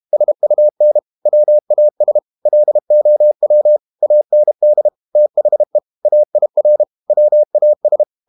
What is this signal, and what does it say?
Morse code, heard as a keyed tone, about 32 wpm, 600 Hz. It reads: SUN WAS LOW AND THE AIR WAS
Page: Never 31wpm | CW med Gnister